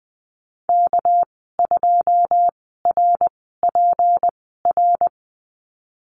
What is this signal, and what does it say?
Morse code, heard as a keyed tone, about 20 wpm, 700 Hz. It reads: K 2 R P R
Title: 200 Lesson 32g - Letter k - Call Signs 20wpm